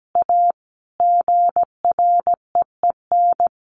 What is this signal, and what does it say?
Morse code, heard as a keyed tone, about 17 wpm, 700 Hz. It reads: A GREEN
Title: A Sound Of Thunder 17wpm